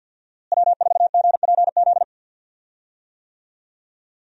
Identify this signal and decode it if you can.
Morse code, heard as a keyed tone, about 52 wpm, 700 Hz. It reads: W4GPZ
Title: USA Call Signs - 5 Characters - Repeated 3x - 50wpm